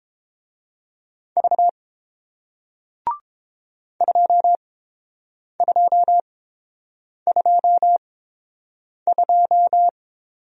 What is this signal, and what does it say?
Morse code, heard as a keyed tone, about 30 wpm, spaced as for 7 wpm, 700 Hz. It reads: V 2222